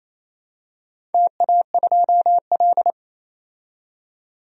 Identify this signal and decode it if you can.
Morse code, heard as a keyed tone, about 28 wpm, 700 Hz. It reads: TA2L
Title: DXCC-100 Call Signs - 4 Characters - 28wpm